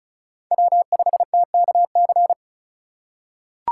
Morse code, heard as a keyed tone, about 36 wpm, 700 Hz. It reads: W5TKC
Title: USA Call Signs - 5 Characters - 2x Character Spacing - 35wpm